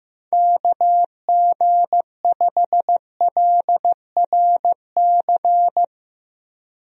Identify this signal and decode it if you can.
Morse code, heard as a keyed tone, about 15 wpm, 700 Hz. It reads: KG5LRC
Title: USA Call Signs - 6 Characters - 15wpm